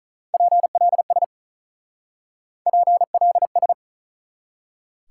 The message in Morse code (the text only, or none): PLS PLS